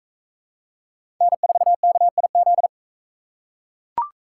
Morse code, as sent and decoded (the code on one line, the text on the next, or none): -. ....- -.- .. --..
N4KIZ